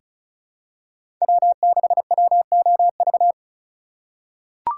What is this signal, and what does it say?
Morse code, heard as a keyed tone, about 36 wpm, 700 Hz. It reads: W6WOV
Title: USA Call Signs - 5 Characters - 3x Character Spacing - 35wpm